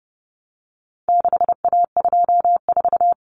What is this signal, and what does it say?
Morse code, heard as a keyed tone, about 30 wpm, 700 Hz. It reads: 6A24
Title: Common Cut Numbers - 4-Digit 30wpm